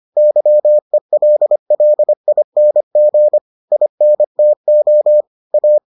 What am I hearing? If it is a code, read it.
Morse code, heard as a keyed tone, about 25 wpm, 600 Hz. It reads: YELLING INTO A